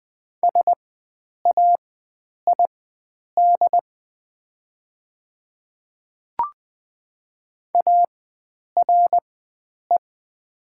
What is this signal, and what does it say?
Morse code, heard as a keyed tone, about 20 wpm, spaced as for 9 wpm, 700 Hz. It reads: SAID ARE
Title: 4x Character Spacing - Top 100 Words - 20wpm